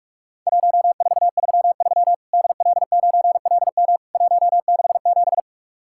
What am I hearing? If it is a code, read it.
Morse code, heard as a keyed tone, about 44 wpm, 700 Hz. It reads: 1433 DL9LM 167
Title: Worked All Europe DX Contest - QTC Exchange - 45wpm